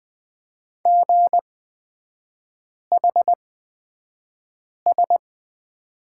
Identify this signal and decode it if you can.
Morse code, heard as a keyed tone, about 20 wpm, spaced as for 5 wpm, 700 Hz. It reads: GHS
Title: 236 Lesson 38e - Letter q - Three Characters 20wpm